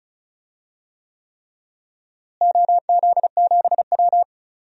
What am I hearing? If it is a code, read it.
Morse code, heard as a keyed tone, about 34 wpm, 700 Hz. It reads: OZ7W
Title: DXCC-100 Call Signs - 4 Characters - Repeated 3x - 35wpm